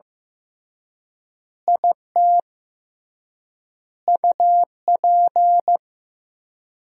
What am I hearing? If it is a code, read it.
Morse code, heard as a keyed tone, about 15 wpm, 700 Hz. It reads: IT UP
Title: 3x Word-Spacing - Sentences from Top 400 Words - 15wpm